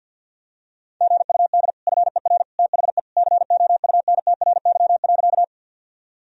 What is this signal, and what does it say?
Morse code, heard as a keyed tone, about 52 wpm, 700 Hz. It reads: GUD FER THE COUNTRY.
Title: Sentences from Top 300 Words and Abbreviations 50wpm